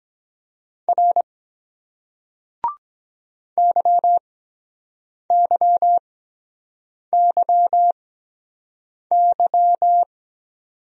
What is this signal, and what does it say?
Morse code, heard as a keyed tone, about 23 wpm, spaced as for 7 wpm, 700 Hz. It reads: R YYYY